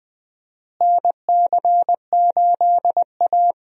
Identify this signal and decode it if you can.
Morse code, heard as a keyed tone, about 20 wpm, 700 Hz. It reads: NC8A
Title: USA Call Signs - 4 Characters - Repeated 3x - 20wpm